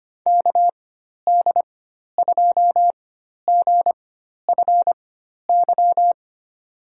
Morse code, heard as a keyed tone, about 25 wpm, spaced as for 12 wpm, 700 Hz. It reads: KD2GFY
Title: USA Call Signs - 6 Characters - 4x Character Spacing - 25wpm